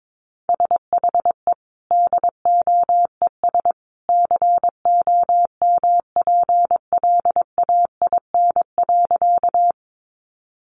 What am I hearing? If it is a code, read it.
Morse code, heard as a keyed tone, about 22 wpm, 700 Hz. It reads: SHE DOES COMPLAIN.
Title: Sentences from Top 1300 Words 22wpm